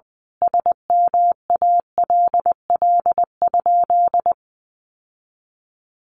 Morse code, heard as a keyed tone, about 20 wpm, 700 Hz. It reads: SMALL?
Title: Sentences from Top 1200 Words 20wpm